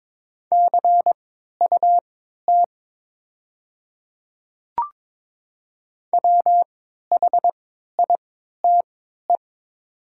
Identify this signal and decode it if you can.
Morse code, heard as a keyed tone, about 22 wpm, spaced as for 12 wpm, 700 Hz. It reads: CUT WHITE
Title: 3x Character Spacing - Top 300 Words - 22wpm